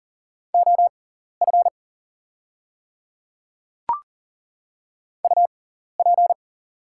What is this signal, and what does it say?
Morse code, heard as a keyed tone, about 40 wpm, spaced as for 14 wpm, 700 Hz. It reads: OF UP
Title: 6x Character Spacing - Top 100 Words - 40wpm